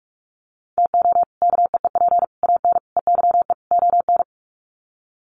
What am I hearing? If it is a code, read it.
Morse code, heard as a keyed tone, about 44 wpm, 700 Hz. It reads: TO KEEP AN EYE ON
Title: Sets of 5 Words Encore 45wpm